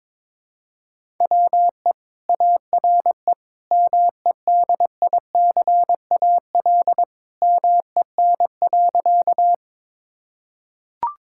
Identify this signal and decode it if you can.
Morse code, heard as a keyed tone, about 22 wpm, 700 Hz. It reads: WE ARE MEDICAL MEN.
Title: Sentences from Top 1000 Words - Review Part 5 - 22wpm